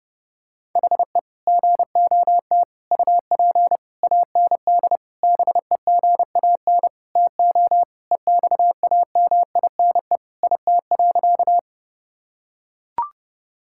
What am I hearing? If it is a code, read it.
Morse code, heard as a keyed tone, about 32 wpm, 700 Hz. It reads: HE GOT UP AND BEGAN TO EXAMINE IT.
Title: Sentences from Top 1600 Words 30wpm